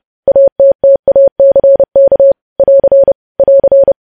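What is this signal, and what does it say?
Morse code, heard as a keyed tone, about 30 wpm, 550 Hz. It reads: ATTACK <AR> <AR>
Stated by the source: Morse code practice of news headlines sent at 30 wpm.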